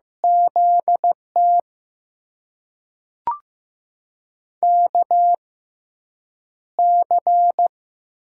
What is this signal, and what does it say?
Morse code, heard as a keyed tone, about 15 wpm, spaced as for 5 wpm, 700 Hz. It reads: ZTKC
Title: USA Call Signs - 6 Characters - 6x Character Spacing - 15wpm